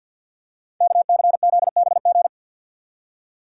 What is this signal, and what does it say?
Morse code, heard as a keyed tone, about 47 wpm, 700 Hz. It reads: KCZBG
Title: Code Groups - 5 Letters - 50wpm